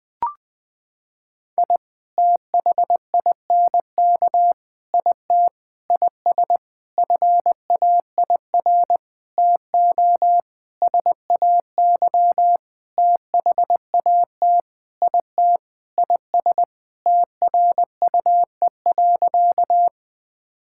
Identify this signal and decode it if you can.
Morse code, heard as a keyed tone, about 21 wpm, 700 Hz. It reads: I THINK IT IS FAIR TO SAY THAT IT IS TRUE.
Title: Sentences with 8-Word Phrases - 20wpm